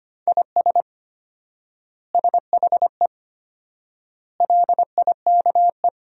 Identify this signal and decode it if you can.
Morse code, heard as a keyed tone, about 25 wpm, 700 Hz. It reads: IS SHE LIKE
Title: 4x Word-Spacing - Sentences from Top 100 Words - 25wpm